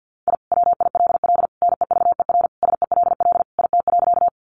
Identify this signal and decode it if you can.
Morse code, heard as a keyed tone, about 50 wpm, 700 Hz. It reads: I WILL NEVER SELL IT.
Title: Sentences from Top 1100 Words 50wpm